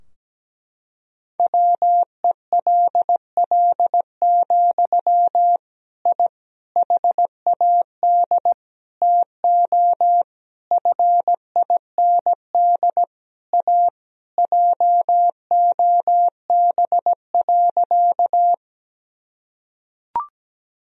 Morse code, heard as a keyed tone, about 17 wpm, 700 Hz. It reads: WELL, I HAD TO FIND A JOB.
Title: Sentences from Top 1000 Words - Review Part 3 - 17wpm